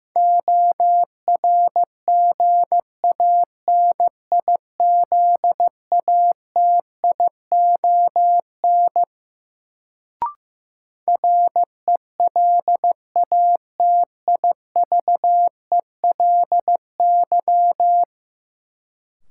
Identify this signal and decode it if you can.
Morse code, heard as a keyed tone, about 15 wpm, 700 Hz. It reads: ORGANIZATION RELATIVELY
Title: Sentences from Top 1100 Words - New Word Drill Alt - 15wpm